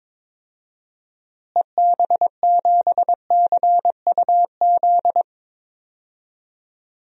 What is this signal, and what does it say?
Morse code, heard as a keyed tone, about 22 wpm, 700 Hz. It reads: EB7CUZ
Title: DXCC-100 Call Signs - 6 Characters - Repeated 2x - 22wpm